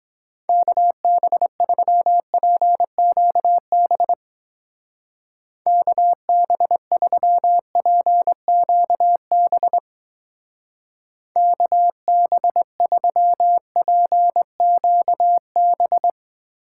Morse code, heard as a keyed tone, about 20 wpm, 700 Hz. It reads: KB3PQB KB3PQB KB3PQB
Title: Speed Racing - USA 2x3 Call Signs 17wpm